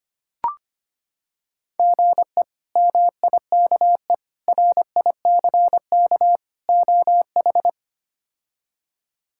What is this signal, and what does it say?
Morse code, heard as a keyed tone, about 25 wpm, 700 Hz. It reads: GE MIKE RICK OH
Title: K1USN SST QSO Practice - 25wpm